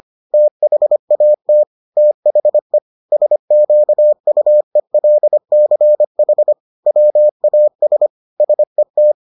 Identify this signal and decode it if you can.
Morse code, heard as a keyed tone, about 25 wpm, 600 Hz. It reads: THAT THE SQUELCH WAS SET